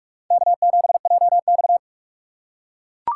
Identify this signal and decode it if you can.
Morse code, heard as a keyed tone, about 41 wpm, 700 Hz. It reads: K7JX